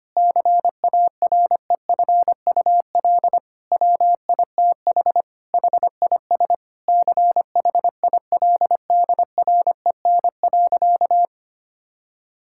Morse code, heard as a keyed tone, about 25 wpm, 700 Hz. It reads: CAREFUL WITH HIS CHILDREN.
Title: Sentences from Top 1600 Words 25wpm